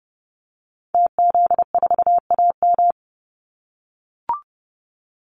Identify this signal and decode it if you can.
Morse code, heard as a keyed tone, about 30 wpm, 700 Hz. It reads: TZ4AM
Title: Call Signs - 30wpm